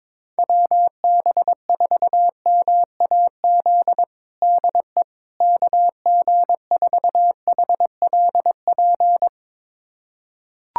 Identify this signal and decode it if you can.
Morse code, heard as a keyed tone, about 22 wpm, 700 Hz. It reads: WB4MAZ DE KG4HLP
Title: USA Callsign DE Callsign - 6 Characters - 22wpm